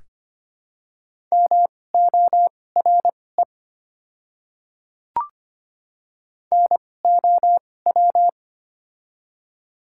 Morse code, heard as a keyed tone, about 25 wpm, spaced as for 18 wpm, 690 Hz. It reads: MORE NOW
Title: 2x Character Spacing - Top 100 Words - 25wpm